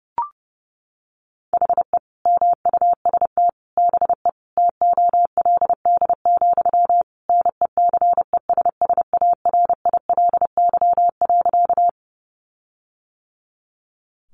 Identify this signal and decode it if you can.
Morse code, heard as a keyed tone, about 30 wpm, 700 Hz. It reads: HE MUST BE TOLD, NECESSARILY.
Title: Sentences from Top 1400 Words 30wpm